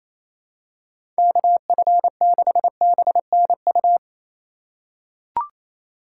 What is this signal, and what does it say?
Morse code, heard as a keyed tone, about 28 wpm, 700 Hz. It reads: KF6BNU
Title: USA Call Signs - 3x Character Spacing - 28wpm